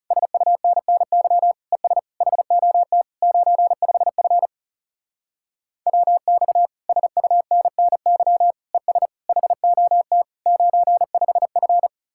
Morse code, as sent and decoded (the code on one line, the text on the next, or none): ... ..- -. -. -.-- / . ... / .... --- - / ----. ..... ..-. / .-- -..- / ... ..- -. -. -.-- / . ... / .... --- - / ----. ..... ..-.
SUNNY ES HOT 95F WX SUNNY ES HOT 95F